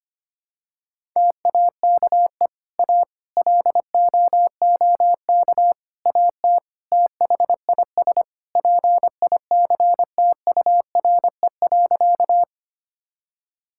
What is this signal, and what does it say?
Morse code, heard as a keyed tone, about 25 wpm, 700 Hz. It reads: TAKE A LOOK AT THIS PICTURE.
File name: practice / 25wpm / Sentences with 4-Word Phrases - 25wpm